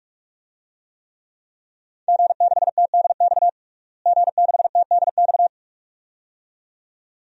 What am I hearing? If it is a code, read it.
Morse code, heard as a keyed tone, about 46 wpm, 700 Hz. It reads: G6TDX G6TDX
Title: DXCC-100 Call Signs - 5 Characters - Repeated 2x - 45wpm